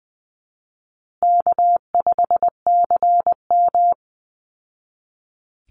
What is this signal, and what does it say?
Morse code, heard as a keyed tone, about 20 wpm, 700 Hz. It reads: K5CM
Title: Easy Call Signs Repeated 3x - 20wpm